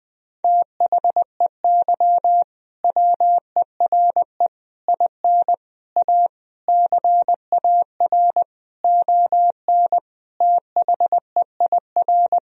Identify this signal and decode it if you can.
Morse code, heard as a keyed tone, about 20 wpm, 700 Hz. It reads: THEY WERE IN A CAR ON THEIR
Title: Sentences from Top 1000 Words - Review Part 1 - 20wpm